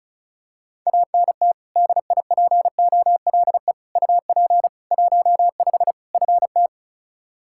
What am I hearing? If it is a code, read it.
Morse code, heard as a keyed tone, about 36 wpm, 700 Hz. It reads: ANT DIPOLE UP 15 FT
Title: Practice QSO - Exchange 2 - Info - 35wpm